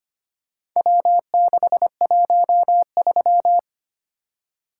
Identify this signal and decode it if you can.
Morse code, heard as a keyed tone, about 25 wpm, 700 Hz. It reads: W613
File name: practice / 25wpm / Four Letter-Number Combinations - 25wpm